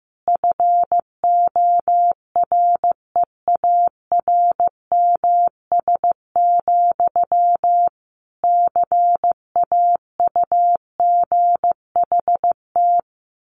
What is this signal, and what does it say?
Morse code, heard as a keyed tone, about 15 wpm, 700 Hz. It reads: FOREARMS, CAUGHT
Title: A Sound Of Thunder 15wpm